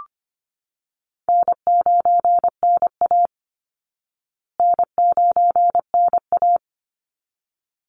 Morse code, heard as a keyed tone, about 25 wpm, 700 Hz. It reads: N9NA N9NA
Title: Easy Call Signs Repeated 2x - 25wpm